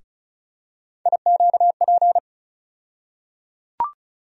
Morse code, heard as a keyed tone, about 36 wpm, 700 Hz. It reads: IQP
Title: USA Call Sign Postfixes - 3 Letters 35wpm